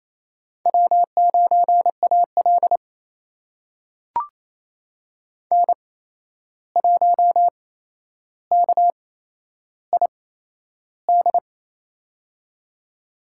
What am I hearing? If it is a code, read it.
Morse code, heard as a keyed tone, about 28 wpm, spaced as for 11 wpm, 700 Hz. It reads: W9AL N1KID
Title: USA Call Signs - 8x Character Spacing - 28wpm